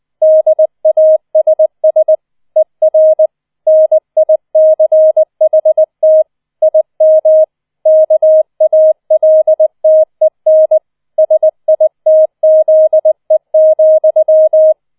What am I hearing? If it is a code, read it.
Morse code, heard as a keyed tone, about 20 wpm, 630 Hz. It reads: DASS ER NICHT IM KALTEN SITZE,
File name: 20wpm